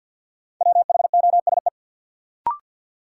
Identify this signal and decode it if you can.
Morse code, heard as a keyed tone, about 53 wpm, 700 Hz. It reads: WHOSE
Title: Sentences from Top 1200 Words - New Word Drill Alt - 50wpm